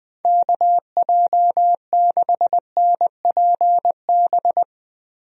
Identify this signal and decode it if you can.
Morse code, heard as a keyed tone, about 21 wpm, 700 Hz. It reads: KJ6NPB
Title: USA Call Signs - 6 Characters - Repeated 3x - 20wpm